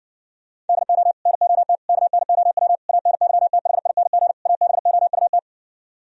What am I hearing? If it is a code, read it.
Morse code, heard as a keyed tone, about 60 wpm, 700 Hz. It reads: DO NOT KNOW ANYTHING ABOUT